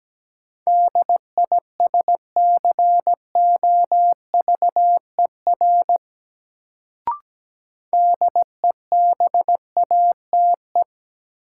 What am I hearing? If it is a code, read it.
Morse code, heard as a keyed tone, about 17 wpm, 700 Hz. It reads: DISCOVER DEBATE
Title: Sentences from Top 1100 Words - New Word Drill - 17wpm